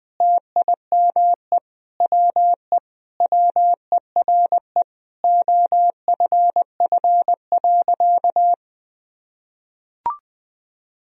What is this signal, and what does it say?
Morse code, heard as a keyed tone, about 20 wpm, 700 Hz. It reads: TIME WE WERE OFF.
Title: Sentences from Top 1000 Words - Review Part 1 - 20wpm